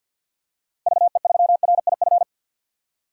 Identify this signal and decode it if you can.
Morse code, heard as a keyed tone, about 52 wpm, 700 Hz. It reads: VE3RIF
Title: DXCC-100 Call Signs - 6 Characters - Repeated 3x - 50wpm